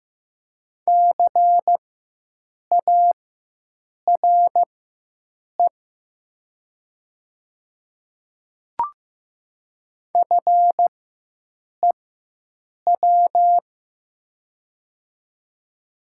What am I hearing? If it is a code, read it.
Morse code, heard as a keyed tone, about 15 wpm, spaced as for 7 wpm, 700 Hz. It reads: CARE FEW
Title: 4x Character Spacing - Top 300 Words - 15wpm